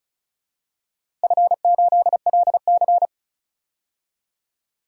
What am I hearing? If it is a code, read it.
Morse code, heard as a keyed tone, about 34 wpm, 700 Hz. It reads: F8LC
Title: DXCC-100 Call Signs - 4 Characters - 35wpm